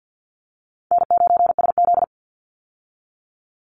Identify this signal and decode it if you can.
Morse code, heard as a keyed tone, about 50 wpm, 700 Hz. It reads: N9SZ
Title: Easy Call Signs - 50wpm